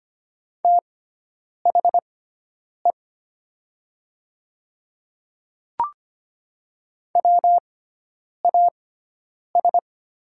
Morse code, heard as a keyed tone, about 25 wpm, spaced as for 9 wpm, 700 Hz. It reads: THE WAS
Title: 6x Character Spacing - Top 100 Words - 25wpm